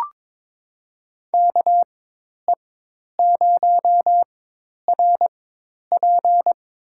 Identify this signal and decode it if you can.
Morse code, heard as a keyed tone, about 22 wpm, spaced as for 10 wpm, 700 Hz. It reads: KE0RP
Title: USA Call Signs - 4x Character Spacing - 22wpm